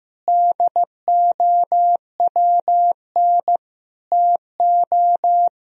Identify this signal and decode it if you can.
Morse code, heard as a keyed tone, about 16 wpm, 700 Hz. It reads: DOWN TO
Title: Sentences from Top 1000 Words - Review Part 2 - 15wpm